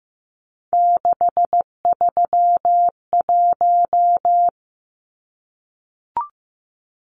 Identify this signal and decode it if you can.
Morse code, heard as a keyed tone, about 15 wpm, 700 Hz. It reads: 631
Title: Numbers - 3-Digit 15wpm